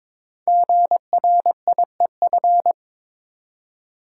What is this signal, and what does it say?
Morse code, heard as a keyed tone, about 22 wpm, 700 Hz. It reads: GRIEF
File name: 5 Letter Words - First 3 Letters Given - 22wpm